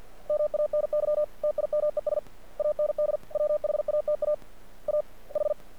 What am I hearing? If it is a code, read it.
Morse code, heard as a keyed tone, about 50 wpm, 600 Hz. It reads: MANY TIMES AND WHATA A H
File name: archived / CRSnet